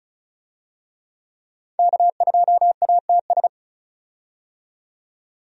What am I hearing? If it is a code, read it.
Morse code, heard as a keyed tone, about 36 wpm, 700 Hz. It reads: K2ATS
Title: USA Call Signs - 5 Characters - 35wpm